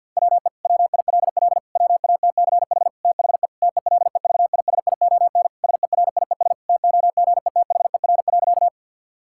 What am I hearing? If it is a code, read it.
Morse code, heard as a keyed tone, about 49 wpm, 700 Hz. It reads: WE WILL WATCH THE TELEVISION SERIES TOGETHER.
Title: Sentences from Top 1000 Words - Review Part 5 - 50wpm